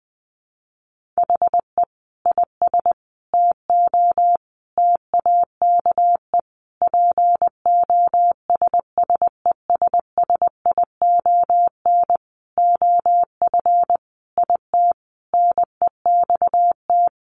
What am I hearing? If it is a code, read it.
Morse code, heard as a keyed tone, about 20 wpm, 700 Hz. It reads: HE IS TO TAKE POSSESSION OF IT NEXT
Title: Sentences from Top 1400 Words 20wpm